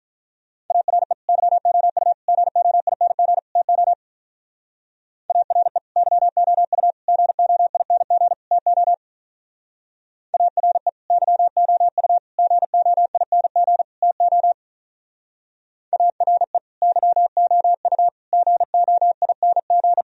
Morse code, heard as a keyed tone, about 42 wpm, 700 Hz. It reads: ARE YOU GOING TO ARE YOU GOING TO ARE YOU GOING TO ARE YOU GOING
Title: Speed-Racing - Sets of 4 Words 35wpm